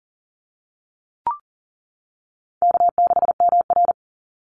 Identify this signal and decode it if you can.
Morse code, heard as a keyed tone, about 40 wpm, 700 Hz. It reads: K6MR
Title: Call Signs - 40wpm